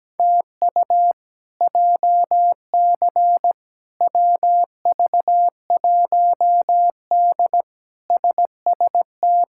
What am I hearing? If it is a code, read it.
Morse code, heard as a keyed tone, about 17 wpm, 700 Hz. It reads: TU JC WV1D SST
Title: K1USN SST QSO Practice - 17wpm